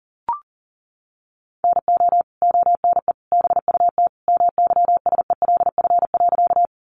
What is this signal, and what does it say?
Morse code, heard as a keyed tone, about 40 wpm, 700 Hz. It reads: NO ONE BUT MYSELF.
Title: Sentences from Top 1300 Words 40wpm